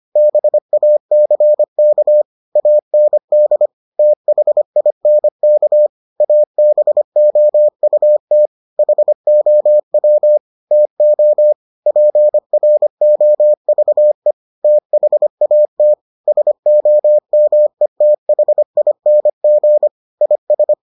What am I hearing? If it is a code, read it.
Morse code, heard as a keyed tone, about 25 wpm, 600 Hz. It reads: BACK AND THINK ABOUT HOW TO PROVE THAT SOMETHING IS